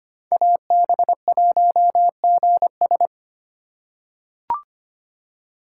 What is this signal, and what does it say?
Morse code, heard as a keyed tone, about 25 wpm, 700 Hz. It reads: AB1GS
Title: USA Call Signs - 5 Characters - 8x Character Spacing - 25wpm